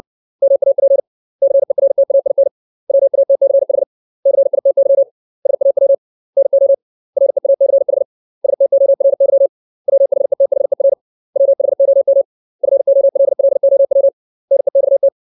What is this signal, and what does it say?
Morse code, heard as a keyed tone, about 60 wpm, 550 Hz. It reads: MAP CERTAIN WATCH CITY SAW NO REACH STORY WHETHER WHOM FOLLOW NEXT
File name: 632 words / 632 Words 60WPM